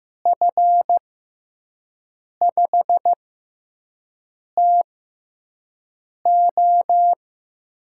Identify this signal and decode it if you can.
Morse code, heard as a keyed tone, about 15 wpm, spaced as for 5 wpm, 700 Hz. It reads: F5TO
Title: USA Call Signs - 5 Characters - 6x Character Spacing - 15wpm